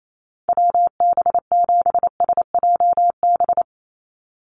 Morse code, heard as a keyed tone, about 28 wpm, 700 Hz. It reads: WB7SJB
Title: USA Call Signs 28wpm